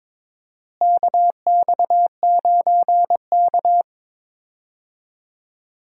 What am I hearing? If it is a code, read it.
Morse code, heard as a keyed tone, about 21 wpm, 700 Hz. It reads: KX9K
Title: USA Call Signs - 4 Characters - Repeated 2x - 22wpm